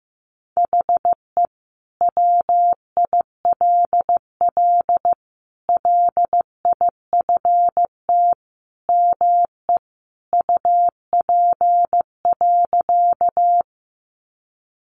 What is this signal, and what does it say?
Morse code, heard as a keyed tone, about 15 wpm, 700 Hz. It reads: HE WILL LIFT ME UP.
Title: Sentences from Top 1000 Words 15wpm